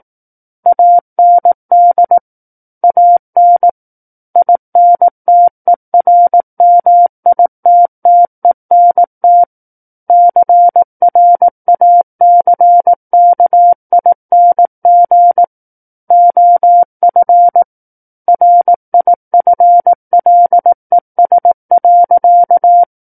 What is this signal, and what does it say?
Morse code, heard as a keyed tone, about 18 wpm, 700 Hz. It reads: AND AN INTERMITTENT CRACKING OF RIFLES.
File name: Book One / 09-Chapter 9 - 18 WPM